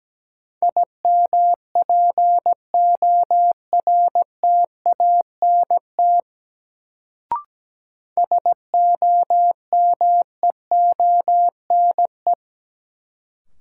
Morse code, heard as a keyed tone, about 17 wpm, 700 Hz. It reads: IMPORTANT SOMEONE
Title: Sentences from Top 600 Words - New Word Drill - 17wpm